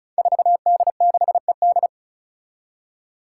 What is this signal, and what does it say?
Morse code, heard as a keyed tone, about 36 wpm, 700 Hz. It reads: 4D6ED
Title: Five Challenging Letter-Number Combinations 35wpm